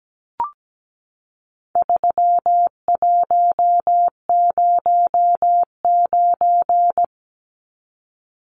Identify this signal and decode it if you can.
Morse code, heard as a keyed tone, about 17 wpm, 700 Hz. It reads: 3109
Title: Numbers - 4-Digit 17wpm